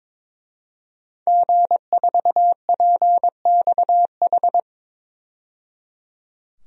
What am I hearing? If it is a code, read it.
Morse code, heard as a keyed tone, about 22 wpm, 700 Hz. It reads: G4PXH
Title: DXCC-100 Call Signs - 5 Characters - 22wpm